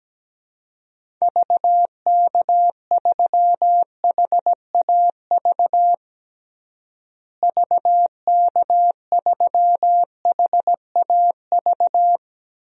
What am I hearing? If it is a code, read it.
Morse code, heard as a keyed tone, about 17 wpm, 700 Hz. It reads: VK3HAV VK3HAV
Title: DXCC-100 Call Signs - 6 Characters - Repeated 2x - 17wpm